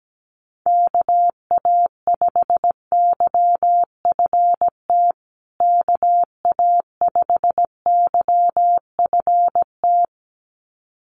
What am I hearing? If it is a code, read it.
Morse code, heard as a keyed tone, about 17 wpm, 700 Hz. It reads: KA5YFT KA5YFT
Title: USA Call Signs Repeated 2x 17wpm